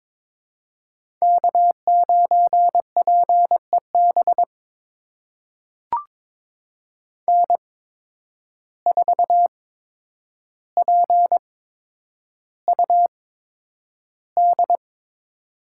Morse code, heard as a keyed tone, about 22 wpm, spaced as for 9 wpm, 700 Hz. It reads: K9PEB N4PUD
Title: USA Call Signs - 8x Character Spacing - 22wpm